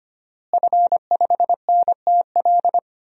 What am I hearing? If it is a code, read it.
Morse code, heard as a keyed tone, about 25 wpm, 700 Hz. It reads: F5NTL
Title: DXCC-100 Call Signs - 5 Characters - Repeated 2x - 25wpm